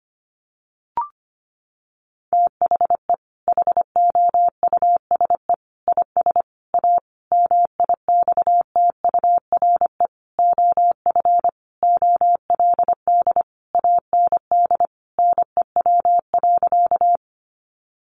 Morse code, heard as a keyed tone, about 25 wpm, 700 Hz. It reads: THE HOUSE IS A MIXTURE OF OLD AND NEW.
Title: Sentences from Top 1500 Words 25wpm